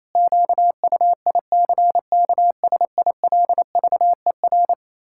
Morse code, heard as a keyed tone, about 28 wpm, 700 Hz. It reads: QUICKSILVER
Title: Pangram Challenge v2 - 28wpm